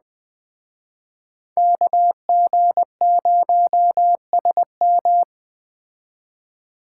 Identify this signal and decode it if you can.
Morse code, heard as a keyed tone, about 20 wpm, 700 Hz. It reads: KG0SM